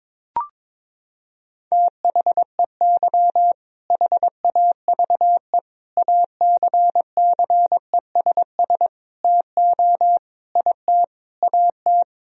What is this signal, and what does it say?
Morse code, heard as a keyed tone, about 22 wpm, 700 Hz. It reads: THEY HAVE ACCESS TO IT AT
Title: Sentences from Top 1000 Words - Review Part 4 - 22wpm